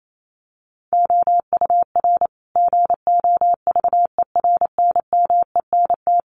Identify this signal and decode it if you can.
Morse code, heard as a keyed tone, about 28 wpm, 700 Hz. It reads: OUR GOVERNMENT
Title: Sentences from Top 1200 Words 28wpm